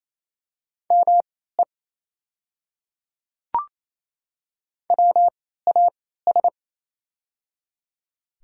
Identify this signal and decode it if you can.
Morse code, heard as a keyed tone, about 27 wpm, spaced as for 16 wpm, 690 Hz. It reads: ME WAS
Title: 3x Character Spacing - Top 200 Words - 28wpm